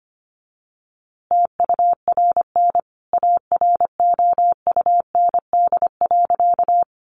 Morse code, heard as a keyed tone, about 25 wpm, 700 Hz. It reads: TURN AROUND.
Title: A Sound Of Thunder 25wpm